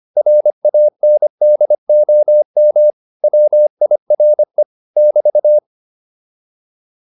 Morse code, heard as a keyed tone, about 25 wpm, 600 Hz. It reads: RANDOM WIRE <BT>